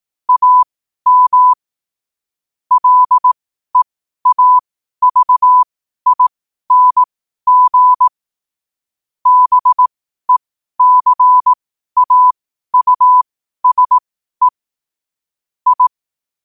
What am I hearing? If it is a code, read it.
Morse code, heard as a keyed tone, about 18 wpm, spaced as for 12 wpm, 1000 Hz. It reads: AM LEAVING BECAUSE I
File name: qotdfiles / 12wpm